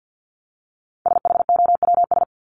Morse code, heard as a keyed tone, about 50 wpm, 700 Hz. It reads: SHOWS
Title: ES:S Suffix - Top 100 Words with Base Words 50wpm